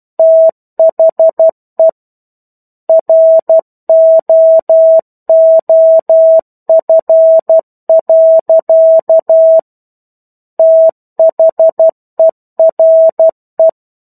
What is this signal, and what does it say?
Morse code, heard as a keyed tone, about 12 wpm, 650 Hz. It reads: THE ROOF. THERE